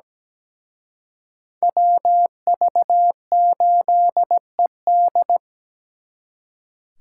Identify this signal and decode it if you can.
Morse code, heard as a keyed tone, about 20 wpm, 700 Hz. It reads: WV8ED
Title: Speed Racing - USA 2x2 Call Signs 17wpm